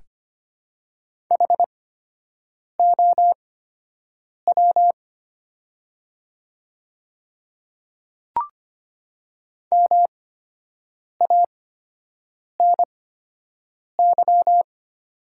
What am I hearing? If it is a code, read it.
Morse code, heard as a keyed tone, about 25 wpm, spaced as for 7 wpm, 700 Hz. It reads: HOW MANY
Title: 8x Character Spacing - Top 100 Words - 25wpm